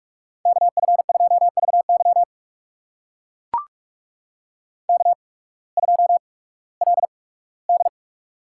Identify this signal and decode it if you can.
Morse code, heard as a keyed tone, about 44 wpm, spaced as for 21 wpm, 700 Hz. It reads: KF2VY K2LD
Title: USA Call Signs - 8x Character Spacing - 45wpm